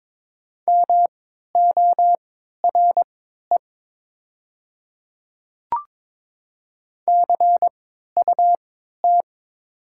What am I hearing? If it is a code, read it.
Morse code, heard as a keyed tone, about 22 wpm, spaced as for 13 wpm, 700 Hz. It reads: MORE CUT
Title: 3x Character Spacing - Top 300 Words - 22wpm